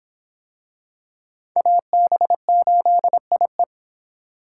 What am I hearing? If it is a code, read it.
Morse code, heard as a keyed tone, about 21 wpm, 700 Hz. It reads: AB8IE
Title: Speed Racing - USA 2x2 Call Signs 17wpm